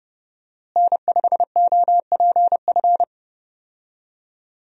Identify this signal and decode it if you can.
Morse code, heard as a keyed tone, about 30 wpm, 700 Hz. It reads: N5OPF
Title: USA Call Signs - 5 Characters - Repeated 2x - 30wpm